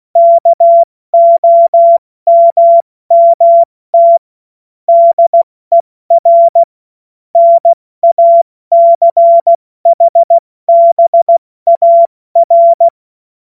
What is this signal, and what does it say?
Morse code, heard as a keyed tone, about 16 wpm, 680 Hz. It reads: KOMMT DER NACHBAR
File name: Morsekurs / Witze / Witz 17 Juni Mai 2025 70 F80